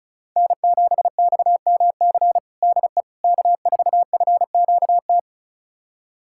Morse code, heard as a keyed tone, about 36 wpm, 700 Hz. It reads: N7XMC DE K4FQT
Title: USA Callsign DE Callsign 35wpm